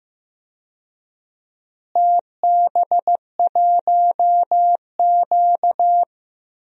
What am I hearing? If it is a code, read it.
Morse code, heard as a keyed tone, about 15 wpm, 700 Hz. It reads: TB1Q
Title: DXCC-100 Call Signs - 4 Characters - Repeated 3x - 15wpm